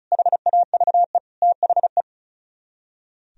Morse code, heard as a keyed tone, about 47 wpm, spaced as for 41 wpm, 700 Hz. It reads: HAVE THE
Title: Speed-Racing - Sets of 4 Words 35wpm